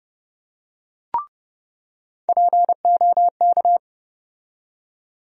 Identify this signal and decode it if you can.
Morse code, heard as a keyed tone, about 32 wpm, 700 Hz. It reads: POK
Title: USA Call Sign Postfixes - 3 Letters 30wpm